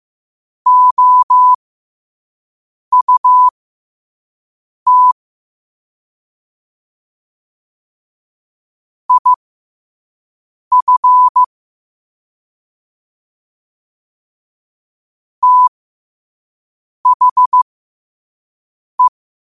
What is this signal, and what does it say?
Morse code, heard as a keyed tone, about 15 wpm, spaced as for 5 wpm, 1000 Hz. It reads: OUT IF THE
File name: qotdfiles / 5wpm